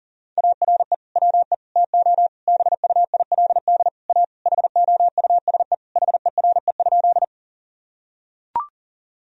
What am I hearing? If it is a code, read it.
Morse code, heard as a keyed tone, about 40 wpm, 700 Hz. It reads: ARE WE TO BUILD A HOUSE HERE?
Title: Sentences from Top 1000 Words - Review Part 1 - 40wpm